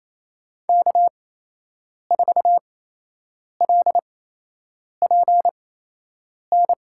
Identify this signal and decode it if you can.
Morse code, heard as a keyed tone, about 27 wpm, spaced as for 8 wpm, 700 Hz. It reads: K4LPN
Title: USA Call Signs - 5 Characters - 8x Character Spacing - 28wpm